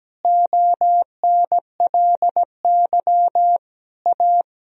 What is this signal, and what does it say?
Morse code, heard as a keyed tone, about 17 wpm, 700 Hz. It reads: ONLY A
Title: Sentences from Top 1000 Words - Review Part 1 - 17wpm